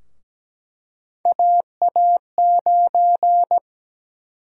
Morse code, heard as a keyed tone, about 17 wpm, 700 Hz. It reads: AA9
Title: USA Call Sign Prefixes - 17wpm